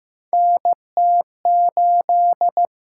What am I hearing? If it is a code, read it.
Morse code, heard as a keyed tone, about 15 wpm, 700 Hz. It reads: NT8
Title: USA Call Signs - 4 Characters - 6x Character Spacing - 15wpm